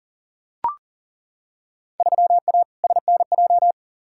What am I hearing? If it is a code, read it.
Morse code, heard as a keyed tone, about 40 wpm, 700 Hz. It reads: 3A SNJ
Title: Field Day Exchanges - 40wpm